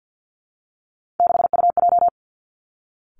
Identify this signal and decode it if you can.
Morse code, heard as a keyed tone, about 50 wpm, 700 Hz. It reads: 6UJ
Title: Three Challenging Letter-Number Combinations 50wpm